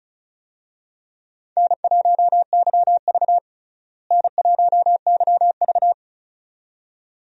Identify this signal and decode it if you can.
Morse code, heard as a keyed tone, about 36 wpm, 700 Hz. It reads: N1YV N1YV
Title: USA Call Signs - 4 Characters - Repeated 2x - 35wpm